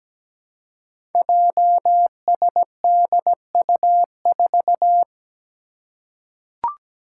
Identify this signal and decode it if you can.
Morse code, heard as a keyed tone, about 17 wpm, 700 Hz. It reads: JSDU4
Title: Five Challenging Letter-Number Combinations 17wpm